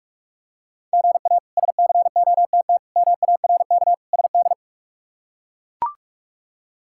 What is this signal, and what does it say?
Morse code, heard as a keyed tone, about 41 wpm, 700 Hz. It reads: GA SCOTT MARK SD
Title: K1USN SST QSO Practice - 45wpm